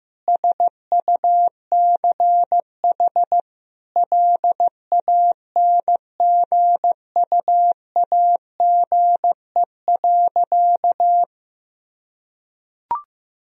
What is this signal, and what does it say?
Morse code, heard as a keyed tone, about 16 wpm, 700 Hz. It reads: SUCH LANGUAGE.
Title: Sentences from Top 1000 Words - Review Part 3 - 15wpm